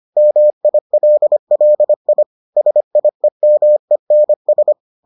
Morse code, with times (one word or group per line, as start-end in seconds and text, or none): 0.2-2.2 MILLI
2.6-4.7 SIEMENS